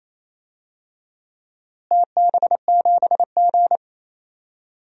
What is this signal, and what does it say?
Morse code, heard as a keyed tone, about 28 wpm, 700 Hz. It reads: TB7G
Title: DXCC-100 Call Signs - 4 Characters - Repeated 2x - 28wpm